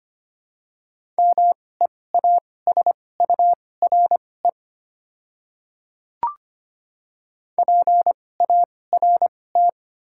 Morse code, heard as a keyed tone, about 25 wpm, spaced as for 18 wpm, 700 Hz. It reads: MEASURE PART
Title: 2x Character Spacing - Top 500 Words - 25wpm